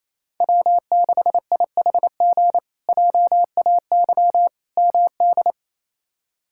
Morse code, heard as a keyed tone, about 28 wpm, 700 Hz. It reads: W6IHG JAY MD
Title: K1USN SST QSO Practice - 28wpm